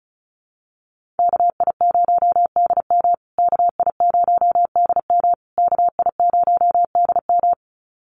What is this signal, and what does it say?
Morse code, heard as a keyed tone, about 34 wpm, 700 Hz. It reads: KI0DM KI0DM KI0DM
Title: USA Call Signs Repeated 3x 35wpm